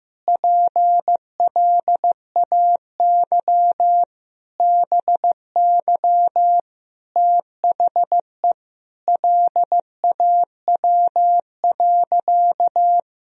Morse code, heard as a keyed tone, about 15 wpm, 700 Hz. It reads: PLAY BY THE LAW.
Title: Sentences from Top 1000 Words - Review Part 3 - 15wpm